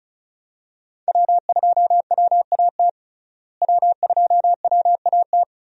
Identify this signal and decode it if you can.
Morse code, heard as a keyed tone, about 36 wpm, 700 Hz. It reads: W2WAT W2WAT
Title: USA Call Signs - 5 Characters - Repeated 3x - 35wpm